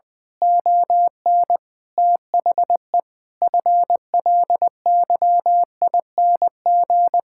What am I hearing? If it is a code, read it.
Morse code, heard as a keyed tone, about 20 wpm, 700 Hz. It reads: ON THE FLYING
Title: Pangram Challenge v2 - 20wpm